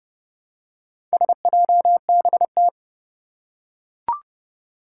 Morse code, heard as a keyed tone, about 29 wpm, 700 Hz. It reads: SJBT
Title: Four Challenging Letter-Number Combinations 30wpm